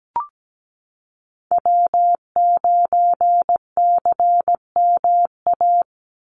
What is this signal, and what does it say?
Morse code, heard as a keyed tone, about 17 wpm, 700 Hz. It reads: W9CMA
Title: USA Call Signs 17wpm